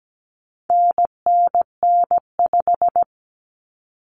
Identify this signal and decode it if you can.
Morse code, heard as a keyed tone, about 17 wpm, 700 Hz. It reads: NNN5
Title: Common Cut Numbers - 4-Digit 17wpm